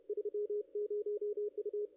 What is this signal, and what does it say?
Morse code, heard as a keyed tone, about 31 wpm, 420 Hz, with radio noise. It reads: 30 U
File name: contests / ss-cw-k5tr-op-n5rz -2200